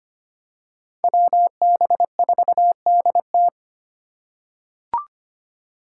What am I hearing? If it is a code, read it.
Morse code, heard as a keyed tone, about 25 wpm, 700 Hz. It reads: WB4DT
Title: USA Call Signs - 5 Characters - 2x Character Spacing - 25wpm